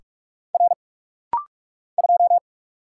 Morse code, heard as a keyed tone, about 44 wpm, 690 Hz. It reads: R 2
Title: Single Letter-Number - Rapid-Fire 45wpm